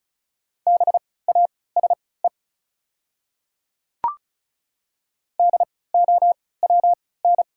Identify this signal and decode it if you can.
Morse code, heard as a keyed tone, about 36 wpm, spaced as for 20 wpm, 700 Hz. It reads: BASE DOWN
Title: 3x Character Spacing - Top 300 Words - 35wpm